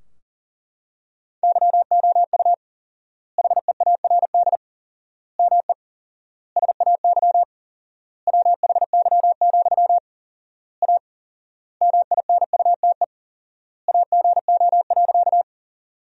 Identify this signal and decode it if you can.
Morse code, heard as a keyed tone, about 40 wpm, 700 Hz. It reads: YOU HEARD ME SAY WHY, A MINUTE AGO.
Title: 4x Word-Spacing - Sentences from Top 500 Words - 40wpm